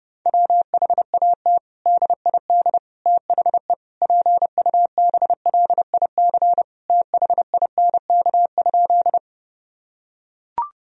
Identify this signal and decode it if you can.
Morse code, heard as a keyed tone, about 31 wpm, 700 Hz. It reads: WHAT DID THE PUBLIC THINK?
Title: Sentences from Top 1000 Words - Review Part 3 - 30wpm